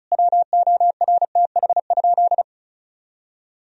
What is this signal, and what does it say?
Morse code, heard as a keyed tone, about 36 wpm, 700 Hz. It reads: WORTH?
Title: Sentences from Top 1000 Words - Review Part 2 - 35wpm